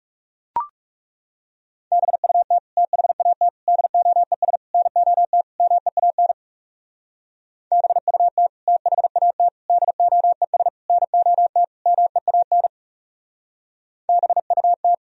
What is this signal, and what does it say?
Morse code, heard as a keyed tone, about 35 wpm, 700 Hz. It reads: BUT THAT DOES NOT MEAN BUT THAT DOES NOT MEAN BUT
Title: Speed-Racing - Sets of 5 Words 30wpm